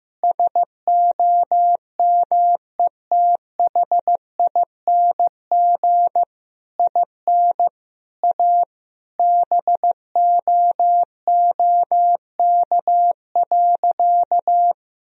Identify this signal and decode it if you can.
Morse code, heard as a keyed tone, about 16 wpm, 700 Hz. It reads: SOMETHING IN A BOOK.
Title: Sentences from Top 1000 Words - Review Part 1 - 15wpm